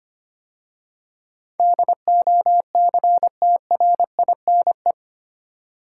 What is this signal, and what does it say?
Morse code, heard as a keyed tone, about 25 wpm, 700 Hz. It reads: DOCTRINE
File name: 8 Letter Words - First 3 Letters Given - 25wpm